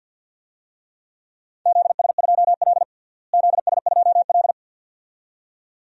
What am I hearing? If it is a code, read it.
Morse code, heard as a keyed tone, about 50 wpm, 700 Hz. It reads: ZS2L ZS2L
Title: DXCC-100 Call Signs - 4 Characters - Repeated 2x - 50wpm